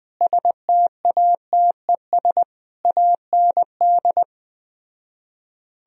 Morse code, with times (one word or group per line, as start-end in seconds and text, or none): 0.2-2.4 STATES
2.9-4.2 AND